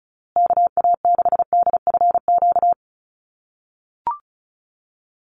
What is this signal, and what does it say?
Morse code, heard as a keyed tone, about 34 wpm, 700 Hz. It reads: KA6DFQ
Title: USA Call Signs Repeated 2x 35wpm